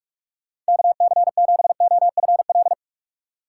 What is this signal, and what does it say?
Morse code, heard as a keyed tone, about 56 wpm, 700 Hz. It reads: KC7OFL
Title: Speed Racing - USA 2x3 Call Signs 45wpm